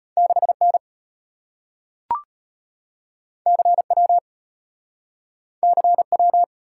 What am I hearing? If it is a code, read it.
Morse code, heard as a keyed tone, about 34 wpm, 700 Hz. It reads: 6N CW CW